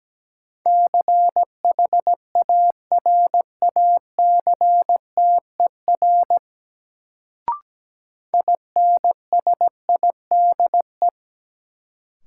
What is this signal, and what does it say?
Morse code, heard as a keyed tone, about 17 wpm, 700 Hz. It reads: CHARACTER INSIDE
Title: Sentences from Top 1200 Words - New Word Drill Alt - 17wpm